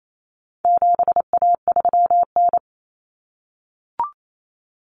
Morse code, heard as a keyed tone, about 28 wpm, 700 Hz. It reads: 7A3N
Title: Common Cut Numbers - 4-Digit 28wpm